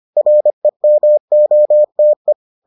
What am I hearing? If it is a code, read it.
Morse code, heard as a keyed tone, about 25 wpm, 600 Hz. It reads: REMOTE